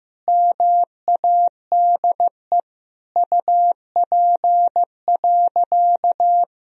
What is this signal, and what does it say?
Morse code, heard as a keyed tone, about 16 wpm, 700 Hz. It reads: MADE UP.
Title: Sentences from Top 1000 Words - Review Part 2 - 15wpm